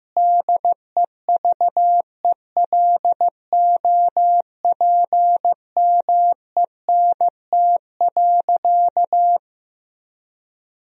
Morse code, heard as a keyed tone, about 15 wpm, 700 Hz. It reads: DEVELOPMENT.
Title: Sentences from Top 1000 Words - Review Part 4 - 15wpm